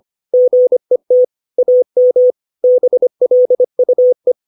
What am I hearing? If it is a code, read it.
Morse code, heard as a keyed tone, about 25 wpm, 500 Hz. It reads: GET AM BLUE
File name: 632 words / 632 Words 25WPM